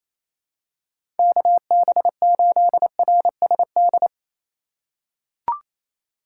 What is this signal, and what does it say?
Morse code, heard as a keyed tone, about 28 wpm, 700 Hz. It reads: KB8RSD
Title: USA Call Signs - 6 Characters - 2x Character Spacing - 28wpm